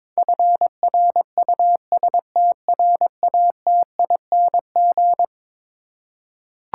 Morse code, heard as a keyed tone, about 22 wpm, 700 Hz. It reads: FRUSTRATING
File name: practice / 22wpm / Eleven Letter Words - 22wpm